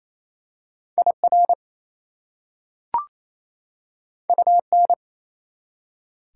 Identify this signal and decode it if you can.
Morse code, heard as a keyed tone, about 28 wpm, 690 Hz. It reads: IR UN
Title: Prefixes and Suffixes 28wpm